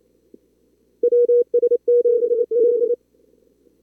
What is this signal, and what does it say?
Morse code, heard as a keyed tone, about 27 wpm, 440 Hz. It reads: WS7D
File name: CWSprint / Feb2022 / k5tr-feb-na-sprint-cw-k5ot -2300